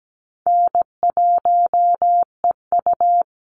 Morse code, heard as a keyed tone, about 17 wpm, 700 Hz. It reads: N1EU
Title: NA SOTA Chaser Callsigns 17wpm